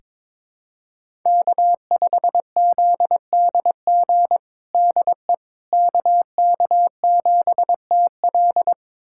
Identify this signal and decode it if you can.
Morse code, heard as a keyed tone, about 23 wpm, 700 Hz. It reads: K5ZDG DE KK7TL
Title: USA Callsign DE Callsign - 5 Characters - 22wpm